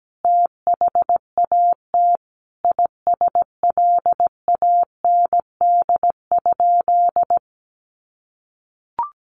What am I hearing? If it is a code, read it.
Morse code, heard as a keyed tone, about 17 wpm, 700 Hz. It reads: THAT ISLAND?
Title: Sentences from Top 500 Words 17wpm